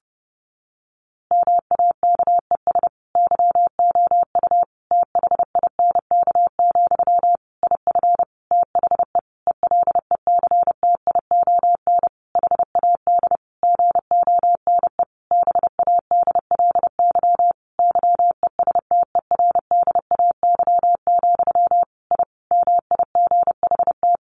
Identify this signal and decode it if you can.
Morse code, heard as a keyed tone, about 30 wpm, 700 Hz. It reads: MAKES YOU THINK, IF THE ELECTION HAD GONE BADLY YESTERDAY, I MIGHT
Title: A Sound Of Thunder 30wpm